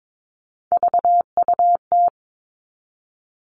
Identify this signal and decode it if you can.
Morse code, heard as a keyed tone, about 22 wpm, 700 Hz. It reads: VUT
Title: All Cut Numbers - 3-Digit 22wpm